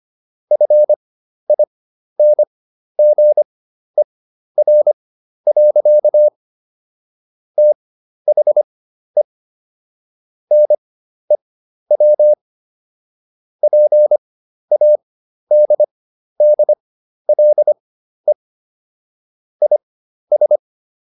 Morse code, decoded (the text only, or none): FINGER. THE NEW PADDLE IS